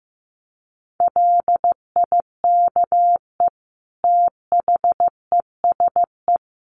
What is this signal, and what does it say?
Morse code, heard as a keyed tone, about 15 wpm, 700 Hz. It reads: LIKE THESE
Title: Sentences from Top 500 Words 15wpm